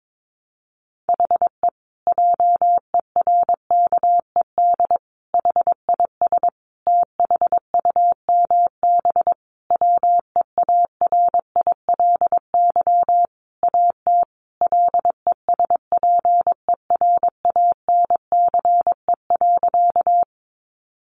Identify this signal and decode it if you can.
Morse code, heard as a keyed tone, about 22 wpm, 700 Hz. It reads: HE JERKED HIS THUMB WEARILY AT LESPERANCE.
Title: A Sound Of Thunder 22wpm